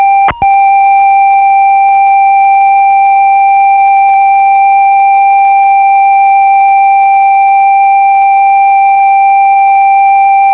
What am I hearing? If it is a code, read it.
Computer data on cassette.